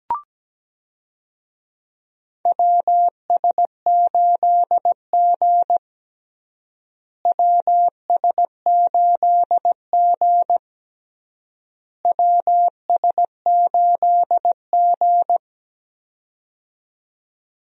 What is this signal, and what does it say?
Morse code, heard as a keyed tone, about 17 wpm, 700 Hz. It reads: WS8G WS8G WS8G
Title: USA Call Signs - 4 Characters - Repeated 3x - 17wpm